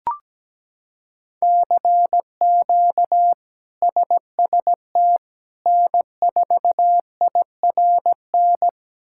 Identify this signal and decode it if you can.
Morse code, heard as a keyed tone, about 17 wpm, 700 Hz. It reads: CQ SST N4IRN
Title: K1USN SST QSO Practice - 17wpm